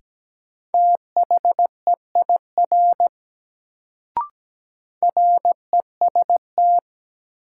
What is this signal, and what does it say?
Morse code, heard as a keyed tone, about 17 wpm, 700 Hz. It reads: THEIR REST
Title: Sets of 2 Words - New Word Drill - 17wpm